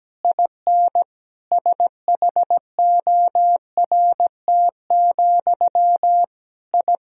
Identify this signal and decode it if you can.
Morse code, heard as a keyed tone, about 17 wpm, 700 Hz. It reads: IN SHORT, I
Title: Sentences from Top 1000 Words - Review Part 2 - 17wpm